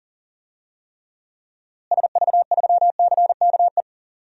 Morse code, heard as a keyed tone, about 40 wpm, 700 Hz. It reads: SV3CKE
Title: DXCC-100 Call Signs - 6 Characters - Repeated 3x - 40wpm